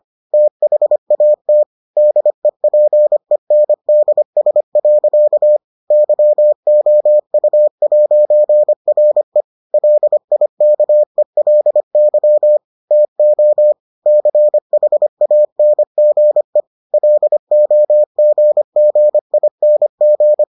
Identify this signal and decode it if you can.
Morse code, heard as a keyed tone, about 25 wpm, 600 Hz. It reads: THAT DEPENDS. YOU'RE LIKELY TO CHANGE LOGGING